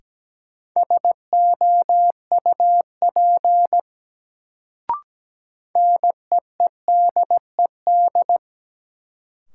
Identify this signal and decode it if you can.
Morse code, heard as a keyed tone, about 17 wpm, 700 Hz. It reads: SOUP NEEDED
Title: Sentences from Top 1000 Words - Review Part 6 - New Word Drill - 17wpm